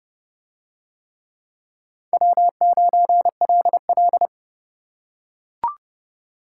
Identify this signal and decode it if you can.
Morse code, heard as a keyed tone, about 29 wpm, 700 Hz. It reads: W9LL